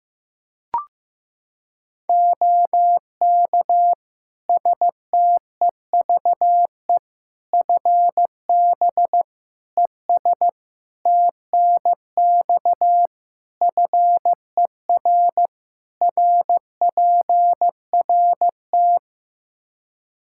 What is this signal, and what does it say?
Morse code, heard as a keyed tone, about 15 wpm, 700 Hz. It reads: OK STEVE FB ES TNX FER RPRT
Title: Practice QSO - Exchange 2 - Info - 15wpm